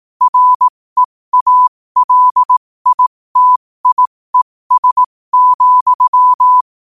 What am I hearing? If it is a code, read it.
Morse code, heard as a keyed tone, about 18 wpm, spaced as for 15 wpm, 1000 Hz. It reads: REALITIES,
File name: qotdfiles / 15wpm